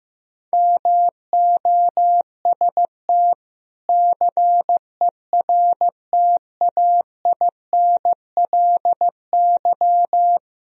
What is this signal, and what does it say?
Morse code, heard as a keyed tone, about 15 wpm, 700 Hz. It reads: MOST CERTAINLY
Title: Sentences from Top 1000 Words - Review Part 1 - 15wpm